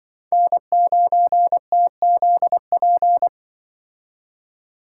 Morse code, heard as a keyed tone, about 23 wpm, 700 Hz. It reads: N9TZP
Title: Speed Racing - USA 1x3 Call Signs 20wpm